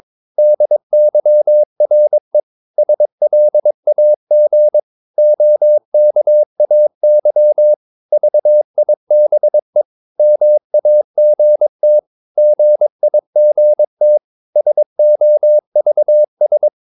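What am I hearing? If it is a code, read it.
Morse code, heard as a keyed tone, about 22 wpm, 600 Hz. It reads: DYRE SLAG OKAY VIBE MAGT GIGT SOVS
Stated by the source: Morse practice of short Danish words at 22 wpm, from CW med Gnister.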